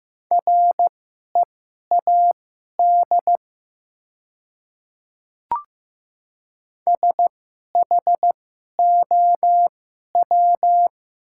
Morse code, heard as a keyed tone, about 15 wpm, spaced as for 11 wpm, 700 Hz. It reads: READ SHOW
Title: 2x Character Spacing - Top 200 Words - 15wpm